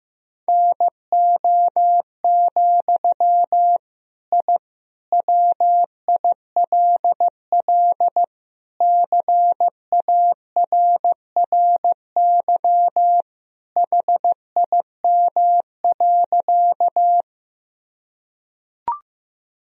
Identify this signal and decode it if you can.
Morse code, heard as a keyed tone, about 15 wpm, 700 Hz. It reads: NO, I WILL CARRY HIM.
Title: Sentences from Top 1000 Words - Review Part 1 - 15wpm